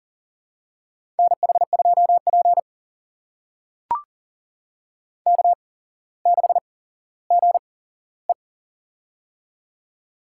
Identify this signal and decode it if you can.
Morse code, heard as a keyed tone, about 40 wpm, spaced as for 17 wpm, 700 Hz. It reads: NH2P K6GE
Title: USA Call Signs - 4 Characters - 8x Character Spacing - 40wpm